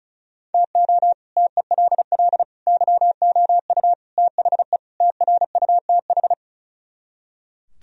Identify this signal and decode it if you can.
Morse code, heard as a keyed tone, about 46 wpm, 700 Hz. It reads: TO TELL YOU THE TRUTH